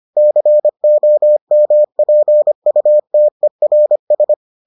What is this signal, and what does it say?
Morse code, heard as a keyed tone, about 25 wpm, 600 Hz. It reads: COMPUTERS